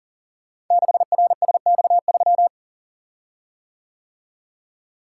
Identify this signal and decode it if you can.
Morse code, heard as a keyed tone, about 43 wpm, 700 Hz. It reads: 6RSX3
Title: Code Groups - 5 Letter:Numbers - 40wpm